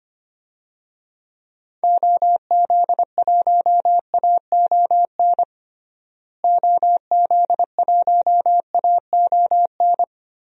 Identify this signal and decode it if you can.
Morse code, heard as a keyed tone, about 25 wpm, 700 Hz. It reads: OZ1AON OZ1AON
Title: DXCC-100 Call Signs - 6 Characters - Repeated 3x - 25wpm